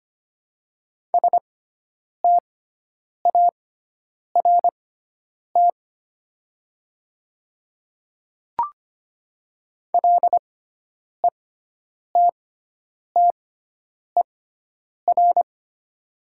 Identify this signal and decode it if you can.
Morse code, heard as a keyed tone, about 25 wpm, spaced as for 9 wpm, 700 Hz. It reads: START LETTER
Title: 6x Character Spacing - Top 300 Words - 25wpm